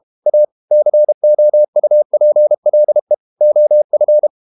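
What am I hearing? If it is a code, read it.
Morse code, heard as a keyed tone, about 32 wpm, 600 Hz. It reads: A COUPLE OF
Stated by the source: Morse code practice at 32 wpm from CW med Gnister, reading Ken Follett's novel Never in English.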